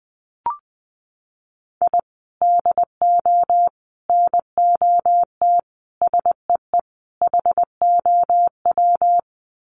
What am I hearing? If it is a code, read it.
Morse code, heard as a keyed tone, about 20 wpm, 700 Hz. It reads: I DO NOT SEE HOW
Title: Sets of 5 Words 20wpm